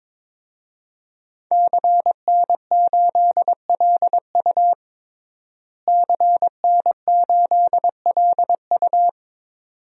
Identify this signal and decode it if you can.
Morse code, heard as a keyed tone, about 22 wpm, 700 Hz. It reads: CN8LU CN8LU
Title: DXCC-100 Call Signs - 5 Characters - Repeated 2x - 22wpm